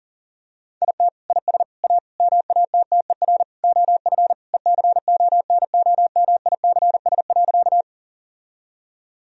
Morse code, heard as a keyed tone, about 40 wpm, 700 Hz. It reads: IT IS A MATTER OF ECONOMICS.
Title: Sentences from Top 1000 Words - Review Part 3 - 40wpm